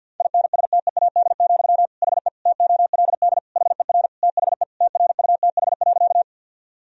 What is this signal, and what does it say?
Morse code, heard as a keyed tone, about 50 wpm, 700 Hz. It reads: INSTEAD, HE TOLD HER THE TRUTH.
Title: Sentences from Top 1000 Words - Review Part 4 - 50wpm